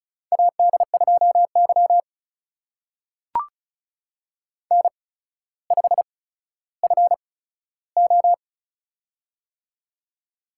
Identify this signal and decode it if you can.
Morse code, heard as a keyed tone, about 36 wpm, spaced as for 15 wpm, 700 Hz. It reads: AD2Y N5FO
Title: USA Call Signs - 4 Characters - 8x Character Spacing - 35wpm